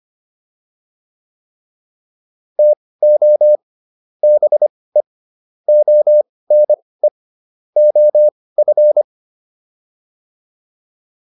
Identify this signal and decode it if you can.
Morse code, heard as a keyed tone, about 25 wpm, spaced as for 18 wpm, 600 Hz. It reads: TO BE ONE OF